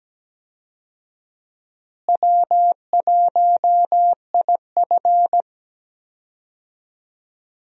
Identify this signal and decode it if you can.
Morse code, heard as a keyed tone, about 17 wpm, 700 Hz. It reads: W1IF
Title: USA Call Signs - 4 Characters - Repeated 3x - 17wpm